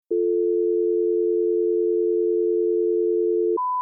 Dial tone (North America), 350 + 440 Hz.